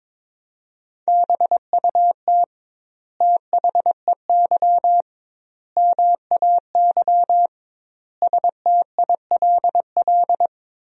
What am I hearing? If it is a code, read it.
Morse code, heard as a keyed tone, about 22 wpm, 700 Hz. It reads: BUT THEY MAY STILL
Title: 2x Word-Spacing - Sentences from Top 300 Words - 22wpm